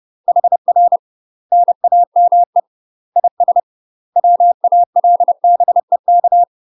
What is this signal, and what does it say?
Morse code, heard as a keyed tone, about 30 wpm, 700 Hz. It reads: HR NAME IS WALBEK
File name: UK MASTER QSOs TRAINING / 30WPM / UK MASTER QSOs 0044